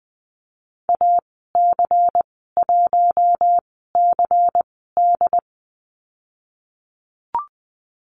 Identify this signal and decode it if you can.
Morse code, heard as a keyed tone, about 20 wpm, spaced as for 15 wpm, 700 Hz. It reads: AC1CD
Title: 179 Lesson 29g - Slash - Call Signs 20wpm